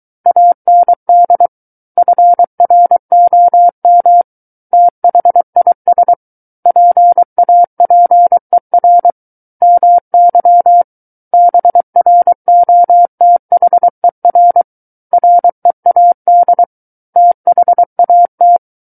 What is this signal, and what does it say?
Morse code, heard as a keyed tone, about 23 wpm, 700 Hz. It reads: AND FROM THIS PAPER MY BROTHER READ THAT
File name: Book One / 14-Chapter 14 - 23 WPM